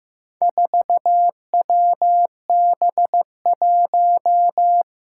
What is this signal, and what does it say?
Morse code, heard as a keyed tone, about 15 wpm, 700 Hz. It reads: 4WB1
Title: Four Challenging Letter-Number Combinations 15wpm